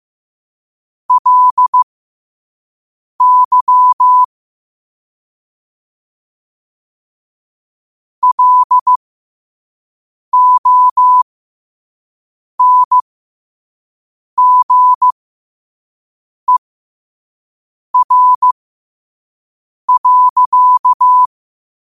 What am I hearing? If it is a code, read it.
Morse code, heard as a keyed tone, about 15 wpm, spaced as for 5 wpm, 1000 Hz. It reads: LY LONGER.
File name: qotdfiles / 5wpm